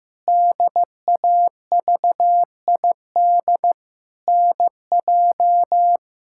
Morse code, heard as a keyed tone, about 16 wpm, 700 Hz. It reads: DAVID NJ
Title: K1USN SST QSO Practice - 15wpm